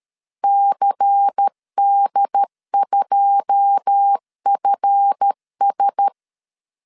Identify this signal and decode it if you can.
Morse code, heard as a keyed tone, about 13 wpm, 800 Hz. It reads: CD2FS
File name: LEZIONI / 10 SIGNS / 10D COMA 12wpm 2544